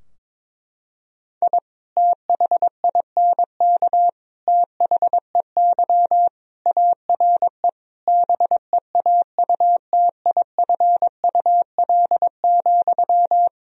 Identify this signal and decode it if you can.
Morse code, heard as a keyed tone, about 22 wpm, 700 Hz. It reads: I THINK THEY ARE BEAUTIFUL,
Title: Sentences from Top 1000 Words - Review Part 4 - 22wpm